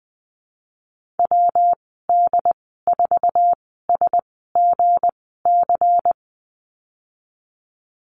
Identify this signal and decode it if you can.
Morse code, heard as a keyed tone, about 20 wpm, spaced as for 15 wpm, 700 Hz. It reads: WD4SGC
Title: 173 Lesson 28g - Number 9 - Call Signs 20wpm